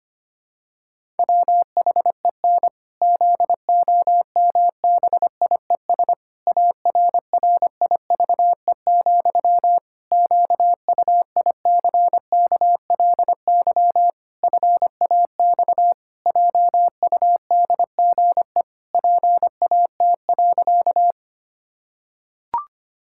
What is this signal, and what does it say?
Morse code, heard as a keyed tone, about 25 wpm, 700 Hz. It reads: WHEN ZOMBIES ARRIVE, QUICKLY FAX JUDGE PAT.
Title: Pangram Challenge v2 - 25wpm